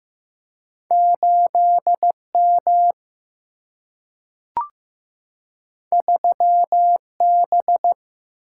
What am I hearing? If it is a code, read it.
Morse code, heard as a keyed tone, about 15 wpm, 700 Hz. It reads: 8M 3B
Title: Single-Letter Single-Number 15wpm